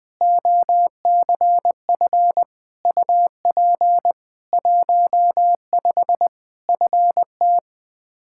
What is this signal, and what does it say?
Morse code, heard as a keyed tone, about 21 wpm, 700 Hz. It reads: OCF UP 15 FT
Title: K1USN SST QSO Practice 20wpm